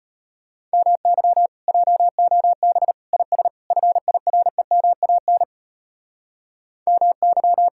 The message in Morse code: MY JOB IS FIREMAN MY

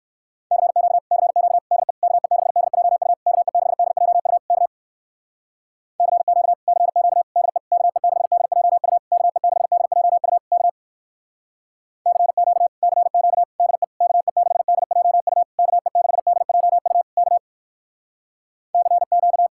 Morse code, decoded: CQ CQ DE KE6DJU KE6DJU K CQ CQ DE KE6DJU KE6DJU K CQ CQ DE KE6DJU KE6DJU K CQ

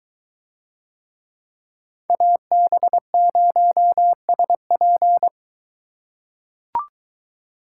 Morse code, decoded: AB0SP

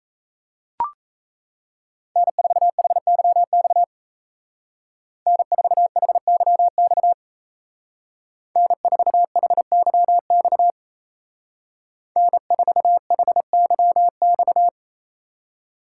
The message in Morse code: N4HYX N4HYX N4HYX N4HYX

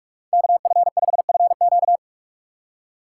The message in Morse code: KF5FQ

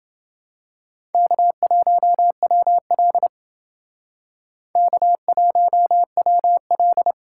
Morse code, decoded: K1WL K1WL

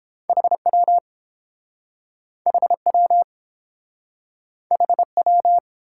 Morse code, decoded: HW HW HW

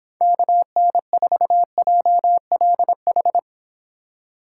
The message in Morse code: KN4JLH

